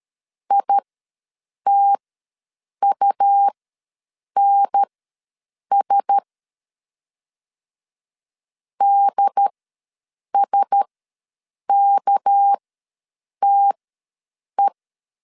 Morse code, heard as a keyed tone, about 13 wpm, spaced as for 7 wpm, 800 Hz. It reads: ITUNS DSKTE
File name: LEZIONI / 6 PKX0 / 6 PK ws 12wpm 3116